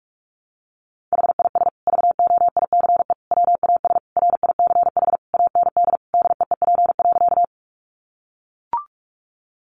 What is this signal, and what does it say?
Morse code, heard as a keyed tone, about 44 wpm, 700 Hz. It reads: HIS VOICE WAS RICH AND DEEP.
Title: Sentences from Top 1100 Words 45wpm